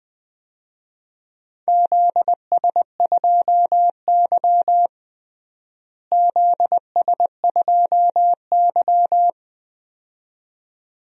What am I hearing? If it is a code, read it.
Morse code, heard as a keyed tone, about 20 wpm, 700 Hz. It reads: ZS2Y ZS2Y
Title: DXCC-100 Call Signs - 4 Characters - Repeated 2x - 20wpm